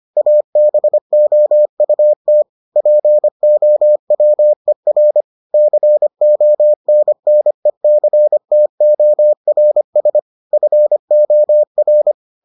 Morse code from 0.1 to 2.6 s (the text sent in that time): ABOUT